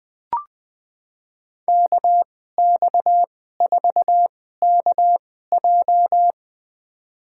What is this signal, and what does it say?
Morse code, heard as a keyed tone, about 21 wpm, spaced as for 15 wpm, 700 Hz. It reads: KX4KJ
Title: USA Call Signs - 2x Character Spacing - 20wpm